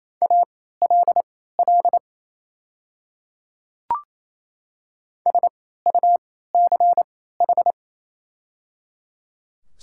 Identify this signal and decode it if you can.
Morse code, heard as a keyed tone, about 27 wpm, spaced as for 16 wpm, 700 Hz. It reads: ALL SUCH
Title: 3x Character Spacing - Top 200 Words - 28wpm